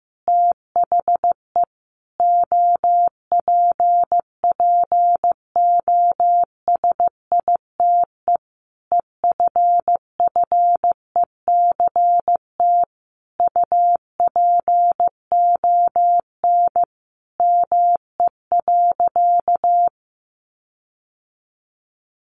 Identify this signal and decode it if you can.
Morse code, heard as a keyed tone, about 15 wpm, 700 Hz. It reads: THE OPPOSITE EFFECT UPON ME.
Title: Sentences from Top 1300 Words 15wpm